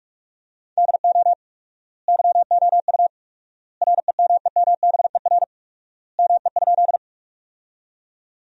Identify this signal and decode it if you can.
Morse code, heard as a keyed tone, about 46 wpm, 700 Hz. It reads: DO YOU REMEMBER ME?
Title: 4x Word-Spacing - Sentences from Top 400 Words - 45wpm